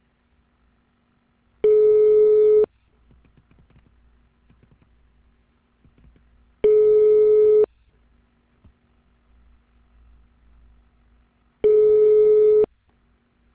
Ringback tone (Europe, 425 Hz).